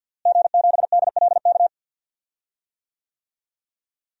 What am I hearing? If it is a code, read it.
Morse code, heard as a keyed tone, about 51 wpm, 700 Hz. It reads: G7DLK
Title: DXCC-100 Call Signs - 5 Characters - Repeated 2x - 50wpm